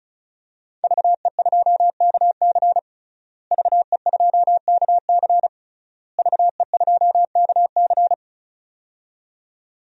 Morse code, heard as a keyed tone, about 34 wpm, 700 Hz. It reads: VE2KC VE2KC VE2KC
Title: DXCC-100 Call Signs - 5 Characters - Repeated 3x - 35wpm